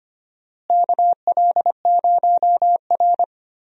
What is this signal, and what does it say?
Morse code, heard as a keyed tone, about 25 wpm, 700 Hz. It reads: KL0R
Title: USA Call Signs - 4 Characters - Repeated 2x - 25wpm